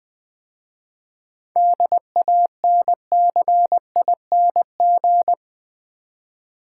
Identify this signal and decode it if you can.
Morse code, heard as a keyed tone, about 20 wpm, 700 Hz. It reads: DANCING
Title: 7 Letter Words - First 3 Letters Given - 20wpm